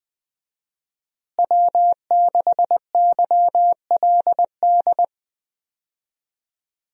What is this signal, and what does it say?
Morse code, heard as a keyed tone, about 21 wpm, 700 Hz. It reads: W6YLD
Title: USA Call Signs - 5 Characters - Repeated 3x - 20wpm